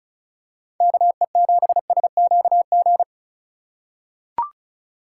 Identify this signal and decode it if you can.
Morse code, heard as a keyed tone, about 36 wpm, 700 Hz. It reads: KE7SQG
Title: USA Call Signs - 6 Characters - 3x Character Spacing - 35wpm